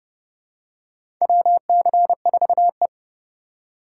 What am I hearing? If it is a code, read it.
Morse code, heard as a keyed tone, about 32 wpm, 700 Hz. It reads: WC4E
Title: USA Call Signs - 4 Characters - 3x Character Spacing - 30wpm